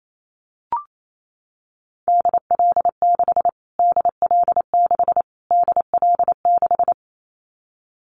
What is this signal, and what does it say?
Morse code, heard as a keyed tone, about 28 wpm, 700 Hz. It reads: DL6 DL6 DL6
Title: Callsign Prefixes for Top 25 DXCC - Repeated 3x - 28wpm